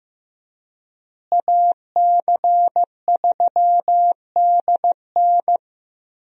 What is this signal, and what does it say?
Morse code, heard as a keyed tone, about 15 wpm, 700 Hz. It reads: AC3DN
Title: USA Call Signs - 5 Characters - Repeated 3x - 15wpm